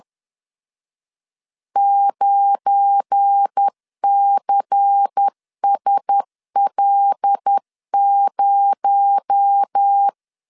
Morse code, heard as a keyed tone, about 10 wpm, 800 Hz. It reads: 9CSL0
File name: LEZIONI / 7 GF9Z / 7 GF9 10wpm 2747